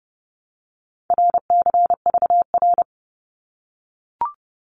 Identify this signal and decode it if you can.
Morse code, heard as a keyed tone, about 30 wpm, 700 Hz. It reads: RCVR
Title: Abbreviations Repeated 3x - 30wpm